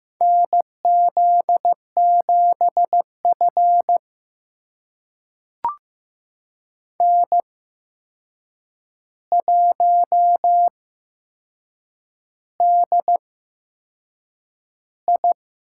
Morse code, heard as a keyed tone, about 15 wpm, spaced as for 6 wpm, 700 Hz. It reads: NZ7F N1DI
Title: USA Call Signs - 4 Characters - 8x Character Spacing - 15wpm